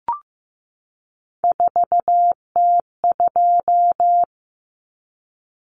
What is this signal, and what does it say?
Morse code, heard as a keyed tone, about 15 wpm, 700 Hz. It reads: 4T2
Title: Common Cut Numbers - 3-Digit 15wpm